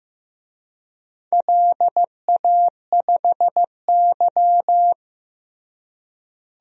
Morse code, heard as a keyed tone, about 16 wpm, 700 Hz. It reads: LA5Y
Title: DXCC-100 Call Signs - 4 Characters - Repeated 2x - 15wpm